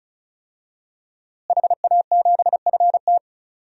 Morse code, heard as a keyed tone, about 36 wpm, 700 Hz. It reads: HA7FT
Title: DXCC-100 Call Signs - 5 Characters - Repeated 3x - 35wpm